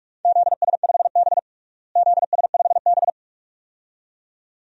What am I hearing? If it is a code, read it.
Morse code, heard as a keyed tone, about 46 wpm, 700 Hz. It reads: ZS5B ZS5B
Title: DXCC-100 Call Signs - 4 Characters - Repeated 2x - 45wpm